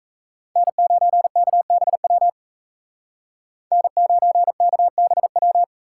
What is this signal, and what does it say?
Morse code, heard as a keyed tone, about 33 wpm, 700 Hz. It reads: N9KBW N9KBW
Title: Speed Racing - USA 1x3 Call Signs 28wpm